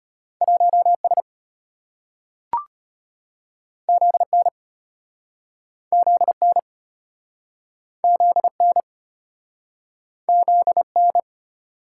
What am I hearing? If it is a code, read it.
Morse code, heard as a keyed tone, about 34 wpm, 700 Hz. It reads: 1S ZN ZN ZN ZN